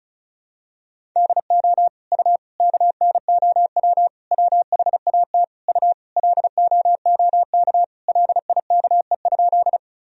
Morse code, heard as a keyed tone, about 34 wpm, 700 Hz. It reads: DO U KNOW WHAT U LOOK LIKE?
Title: Sentences from Top 100 Words and Abbreviations 35wpm